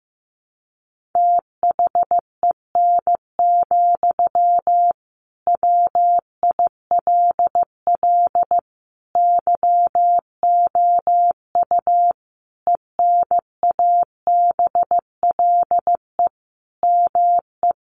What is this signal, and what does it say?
Morse code, heard as a keyed tone, about 15 wpm, 700 Hz. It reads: THEN, WILL YOU ENABLE ME
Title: Sentences from Top 1500 Words 15wpm